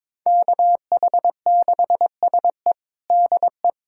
Morse code, heard as a keyed tone, about 22 wpm, 700 Hz. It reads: KH6SE DE
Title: USA Callsign DE Callsign - 5 Characters - 22wpm